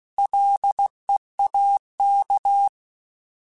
Morse code, heard as a keyed tone, about 16 wpm, 800 Hz. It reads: LEAK
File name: Alphabetical / Alphabetical mp3s / Ls @15